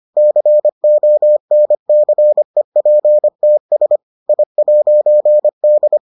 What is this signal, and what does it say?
Morse code, heard as a keyed tone, about 25 wpm, 600 Hz. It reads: CONCEPTS I'D